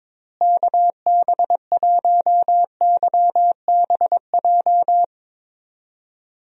Morse code, heard as a keyed tone, about 21 wpm, 700 Hz. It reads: KB1YBJ
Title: USA Call Signs - 6 Characters - Repeated 2x - 22wpm